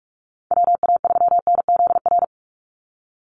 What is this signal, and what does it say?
Morse code, heard as a keyed tone, about 46 wpm, 700 Hz. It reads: WA3NZR
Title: Call Signs - 45wpm